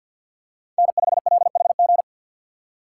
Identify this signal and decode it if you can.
Morse code, heard as a keyed tone, about 47 wpm, 700 Hz. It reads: N5LHG
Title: USA Call Signs - 5 Characters - 2x Character Spacing - 50wpm